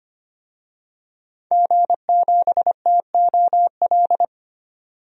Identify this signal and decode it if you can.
Morse code, heard as a keyed tone, about 25 wpm, 700 Hz. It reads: G7TOL
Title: DXCC-100 Call Signs - 5 Characters - 25wpm